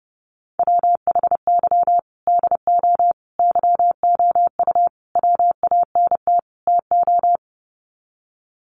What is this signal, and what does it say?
Morse code, heard as a keyed tone, about 30 wpm, 700 Hz. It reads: WHY DO YOU WANT TO
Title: Sets of 5 Words Encore2 30wpm